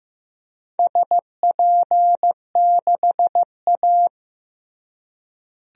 Morse code, heard as a keyed tone, about 15 wpm, 700 Hz. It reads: SP6A
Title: DXCC-100 Call Signs - 4 Characters - Repeated 2x - 15wpm